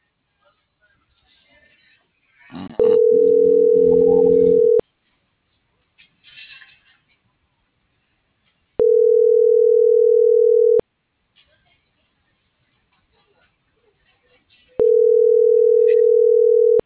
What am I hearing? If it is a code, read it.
Ringback tone (North America), 440 + 480 Hz.